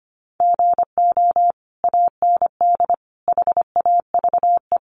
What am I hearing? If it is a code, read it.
Morse code, heard as a keyed tone, about 25 wpm, 700 Hz. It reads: GO AND HAVE
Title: Sentences from Top 600 Words 25wpm